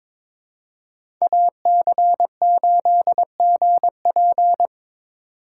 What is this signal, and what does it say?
Morse code, heard as a keyed tone, about 26 wpm, 700 Hz. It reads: AC8GP
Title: Speed Racing - USA 2x2 Call Signs 22wpm